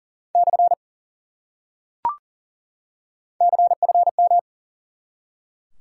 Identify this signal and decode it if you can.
Morse code, heard as a keyed tone, about 40 wpm, 700 Hz. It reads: / CFM
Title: Top 90 QSO Elements 40wpm